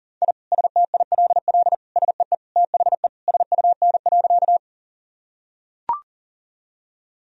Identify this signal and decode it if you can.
Morse code, heard as a keyed tone, about 40 wpm, 700 Hz. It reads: I STILL SEE THE SUN.
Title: Sentences from Top 1000 Words - Review Part 1 - 40wpm